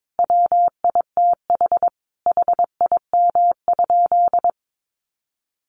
Morse code, heard as a keyed tone, about 22 wpm, 700 Hz. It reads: WITH HIM?
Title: Sentences from Top 600 Words 22wpm